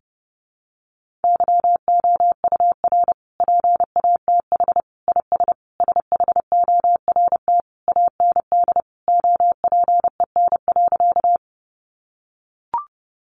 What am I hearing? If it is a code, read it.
Morse code, heard as a keyed tone, about 28 wpm, 700 Hz. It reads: YOUR PATH IS SHORT AND OPEN.
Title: Sentences from Top 1400 Words 30wpm